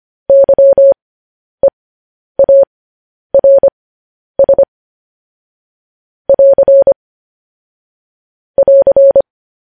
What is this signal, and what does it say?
Morse code, heard as a keyed tone, about 25 wpm, spaced as for 10 wpm, 550 Hz. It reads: YEARS <AR> <AR>
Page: news headlines in morse code at 10 wpm